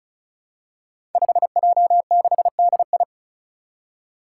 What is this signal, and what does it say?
Morse code, heard as a keyed tone, about 34 wpm, 700 Hz. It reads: 5J6DI
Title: Five Challenging Letter-Number Combinations 35wpm